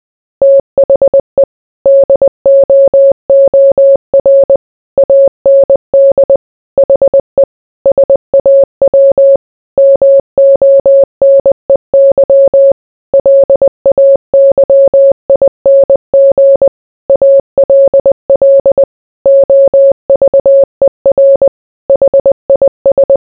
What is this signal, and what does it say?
Morse code, heard as a keyed tone, about 20 wpm, 550 Hz. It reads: THE DOOR AND HE SAW MONEY LAYING ALL OVER HIS